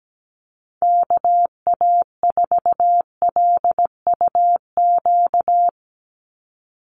Morse code, heard as a keyed tone, about 17 wpm, 700 Hz. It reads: KA4LUQ
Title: USA Call Signs 17wpm